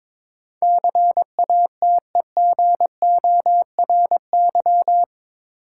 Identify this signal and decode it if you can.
Morse code, heard as a keyed tone, about 22 wpm, 700 Hz. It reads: CATEGORY
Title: Sentences from Top 1000 Words - Review Part 7 - New Word Drill - 22wpm